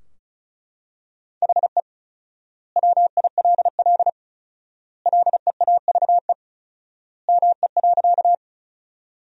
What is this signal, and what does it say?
Morse code, heard as a keyed tone, about 34 wpm, 700 Hz. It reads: HE WILL LEAVE ME.
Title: 4x Word-Spacing - Sentences from Top 400 Words - 35wpm